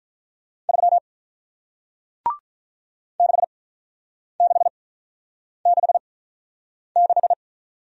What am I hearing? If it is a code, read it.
Morse code, heard as a keyed tone, about 48 wpm, spaced as for 9 wpm, 700 Hz. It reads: 3 6666